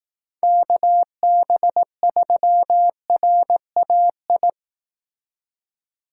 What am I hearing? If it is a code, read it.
Morse code, heard as a keyed tone, about 18 wpm, 700 Hz. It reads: KB3RAI
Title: Speed Racing - USA Call Signs 15wpm